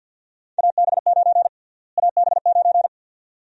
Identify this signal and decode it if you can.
Morse code, heard as a keyed tone, about 52 wpm, 700 Hz. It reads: AB9 AB9
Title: USA Call Sign Prefixes - Repeated 2x - 50wpm